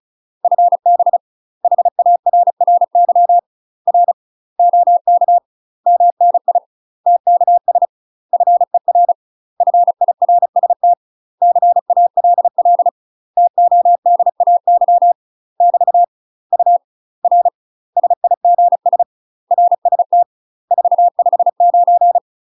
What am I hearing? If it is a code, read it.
Morse code, heard as a keyed tone, about 36 wpm, 700 Hz. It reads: FB HARRY R OK MNI TKS FER FIRST CALL TODAY <BT> U R SIGS RST 459
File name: UK MASTER QSOs TRAINING / 35WPM / UK MASTER QSOs 0035